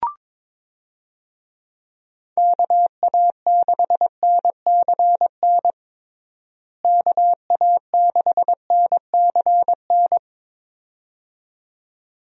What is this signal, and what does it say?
Morse code, heard as a keyed tone, about 22 wpm, 700 Hz. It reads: KA6NCN KA6NCN
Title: USA Call Signs - 6 Characters - Repeated 2x - 22wpm